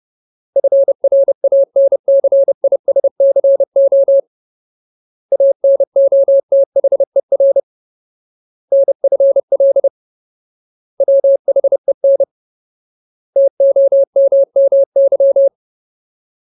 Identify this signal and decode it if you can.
Morse code, heard as a keyed tone, about 30 wpm, 550 Hz. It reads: FRANCISCO ANOTHER NFL WHEN TOMMY
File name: Ultimate Word List / 30WPM Ultimate